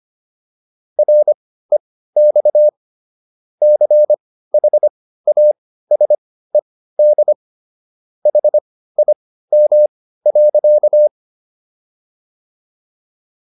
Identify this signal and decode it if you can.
Morse code, heard as a keyed tone, about 25 wpm, spaced as for 15 wpm, 600 Hz. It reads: REX CHASED HIM.